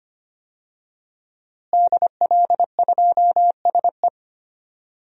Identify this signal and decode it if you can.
Morse code, heard as a keyed tone, about 25 wpm, 700 Hz. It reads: DL2SE
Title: DXCC-100 Call Signs - 5 Characters - Repeated 2x - 25wpm